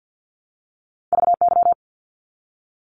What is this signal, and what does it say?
Morse code, heard as a keyed tone, about 50 wpm, 700 Hz. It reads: VY
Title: Abbreviations 50wpm